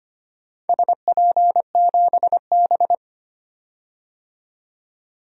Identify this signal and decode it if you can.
Morse code, heard as a keyed tone, about 25 wpm, 700 Hz. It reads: SP7B
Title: DXCC-100 Call Signs - 4 Characters - 25wpm